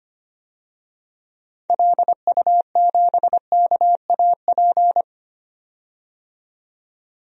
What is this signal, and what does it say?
Morse code, heard as a keyed tone, about 25 wpm, 700 Hz. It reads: LU7KAP
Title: DXCC-100 Call Signs - 6 Characters - 25wpm